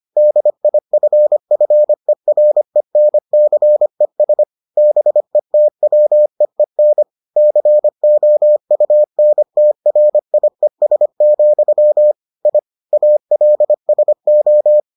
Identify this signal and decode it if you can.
Morse code, heard as a keyed tone, about 25 wpm, 600 Hz. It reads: DIFFERENCES BETWEEN COUNTRIES, I ALSO